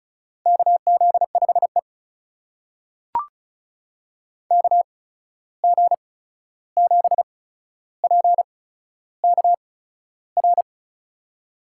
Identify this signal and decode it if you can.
Morse code, heard as a keyed tone, about 35 wpm, spaced as for 13 wpm, 700 Hz. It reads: KZ5E KG7PKR
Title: USA Call Signs - 8x Character Spacing - 35wpm